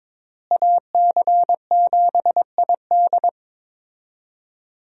Morse code, heard as a keyed tone, about 22 wpm, 700 Hz. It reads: AC7ID
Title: USA Call Signs - 5 Characters - Repeated 2x - 22wpm